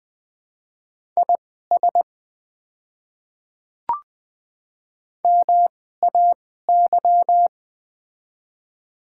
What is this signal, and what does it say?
Morse code, heard as a keyed tone, about 20 wpm, spaced as for 14 wpm, 700 Hz. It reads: IS MAY
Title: 2x Character Spacing - Top 100 Words - 20wpm